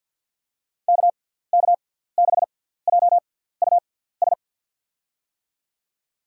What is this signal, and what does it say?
Morse code, heard as a keyed tone, about 47 wpm, spaced as for 17 wpm, 700 Hz. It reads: KK6JUS
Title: USA Call Signs - 6 Characters - 6x Character Spacing - 50wpm